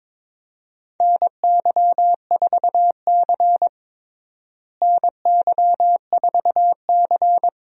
Morse code, heard as a keyed tone, about 22 wpm, 700 Hz. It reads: NY4C NY4C
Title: USA Call Signs - 4 Characters - Repeated 3x - 22wpm